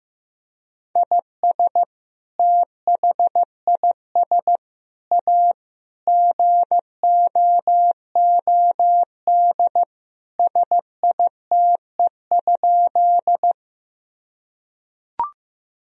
Morse code, heard as a keyed tone, about 15 wpm, 700 Hz. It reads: IS THIS A GOOD SITE?
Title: Sentences from Top 1000 Words - Review Part 4 - 15wpm